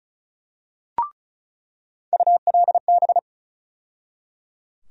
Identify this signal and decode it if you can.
Morse code, heard as a keyed tone, about 34 wpm, 700 Hz. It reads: ULB
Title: USA Call Sign Postfixes - 3 Letters 35wpm